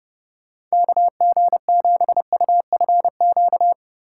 Morse code, heard as a keyed tone, about 30 wpm, 700 Hz. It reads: KG7UFQ